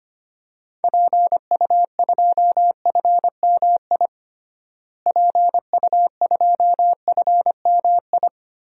Morse code, heard as a keyed tone, about 25 wpm, 700 Hz. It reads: PU2FMI PU2FMI
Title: DXCC-100 Call Signs - 6 Characters - Repeated 3x - 25wpm